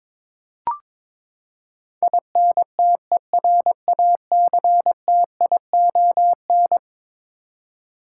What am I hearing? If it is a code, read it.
Morse code, heard as a keyed tone, about 22 wpm, 700 Hz. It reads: INTERACTION
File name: practice / 22wpm / Eleven Letter Words - 22wpm